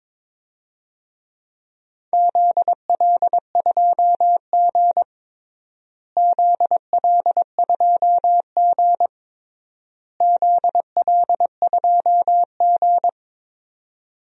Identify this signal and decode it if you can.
Morse code, heard as a keyed tone, about 22 wpm, 700 Hz. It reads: ZL2G ZL2G ZL2G
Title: DXCC-100 Call Signs - 4 Characters - Repeated 3x - 22wpm